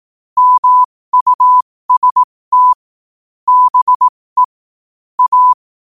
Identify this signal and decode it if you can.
Morse code, heard as a keyed tone, about 18 wpm, spaced as for 15 wpm, 1000 Hz. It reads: MUST BE A
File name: qotdfiles / 15wpm